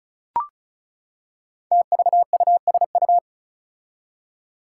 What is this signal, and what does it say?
Morse code, heard as a keyed tone, about 36 wpm, 700 Hz. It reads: TVUSU
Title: Five Challenging Letter-Number Combinations 35wpm